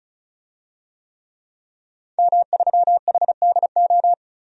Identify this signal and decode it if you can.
Morse code, heard as a keyed tone, about 36 wpm, 700 Hz. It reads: M3HDO
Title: DXCC-100 Call Signs - 5 Characters - 35wpm